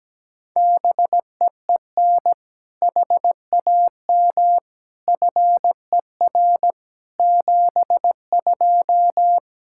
Morse code, heard as a keyed tone, about 17 wpm, 700 Hz. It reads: BEEN HAM FER 72
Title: Practice QSO - Exchange 3 - More Info - 17wpm